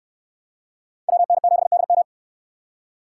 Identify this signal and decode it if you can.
Morse code, heard as a keyed tone, about 69 wpm, 700 Hz. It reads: WN7DM